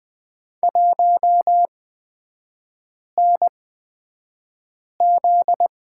Morse code, heard as a keyed tone, about 20 wpm, spaced as for 5 wpm, 700 Hz. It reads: 1NZ
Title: 243 Lesson 39e - Letter z - Three Characters 20wpm